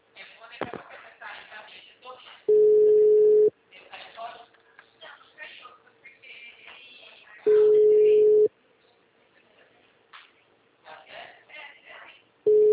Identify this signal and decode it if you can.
Ringback tone (Europe, 425 Hz), with speech.